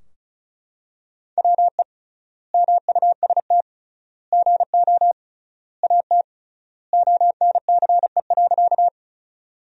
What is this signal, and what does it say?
Morse code, heard as a keyed tone, about 36 wpm, 700 Hz. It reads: WE MUST GO AT ONCE.
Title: 3x Word-Spacing - Sentences from Top 300 Words - 35wpm